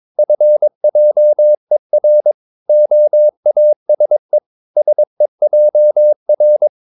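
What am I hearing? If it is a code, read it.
Morse code, heard as a keyed tone, about 22 wpm, 600 Hz. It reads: FJER OASE SEJR
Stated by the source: Morse practice of short Danish words at 22 wpm, from CW med Gnister.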